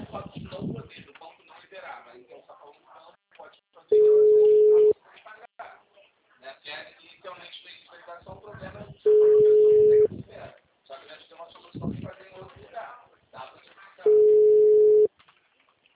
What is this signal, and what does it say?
Ringback tone (Europe, 425 Hz), with speech.
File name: out- -2033- - - .10892